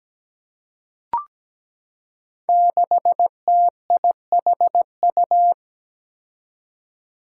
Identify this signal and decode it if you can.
Morse code, heard as a keyed tone, about 17 wpm, 700 Hz. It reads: 6TIHU
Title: Five Challenging Letter-Number Combinations 17wpm